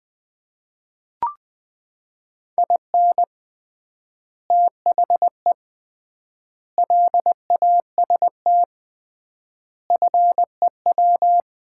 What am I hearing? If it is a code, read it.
Morse code, heard as a keyed tone, about 20 wpm, 700 Hz. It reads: IN THE LAST FEW
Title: 3x Word-Spacing - Sets of 5 Words - 20wpm